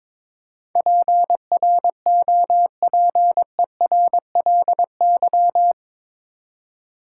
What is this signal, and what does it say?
Morse code, heard as a keyed tone, about 22 wpm, 700 Hz. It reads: PROPERLY
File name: practice / 22wpm / Eight Letter Words - 22wpm